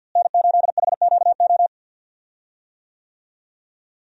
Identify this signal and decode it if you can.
Morse code, heard as a keyed tone, about 52 wpm, 700 Hz. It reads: N8HQO
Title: USA Call Signs - 5 Characters - Repeated 3x - 50wpm